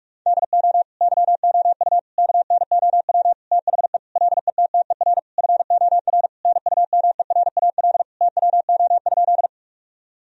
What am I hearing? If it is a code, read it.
Morse code, heard as a keyed tone, about 46 wpm, 700 Hz. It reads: DO YOU KNOW THE LETTER FOR NUMERAL TWO?
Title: Sentences from Top 1000 Words - Review Part 2 - 45wpm